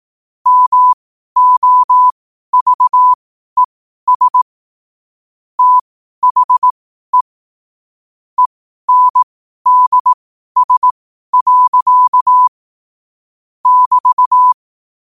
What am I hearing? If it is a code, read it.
Morse code, heard as a keyed tone, about 18 wpm, spaced as for 12 wpm, 1000 Hz. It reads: MOVES THE ENDS. <BT>
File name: qotdfiles / 12wpm